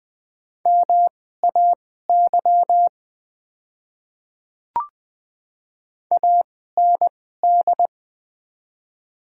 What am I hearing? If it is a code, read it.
Morse code, heard as a keyed tone, about 20 wpm, spaced as for 14 wpm, 700 Hz. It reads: MAY AND
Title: 2x Character Spacing - Top 100 Words - 20wpm